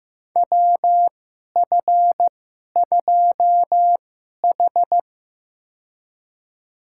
Morse code, heard as a keyed tone, about 15 wpm, spaced as for 11 wpm, 700 Hz. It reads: WF2H
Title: USA Call Signs - 4 Characters - 2x Character Spacing - 15wpm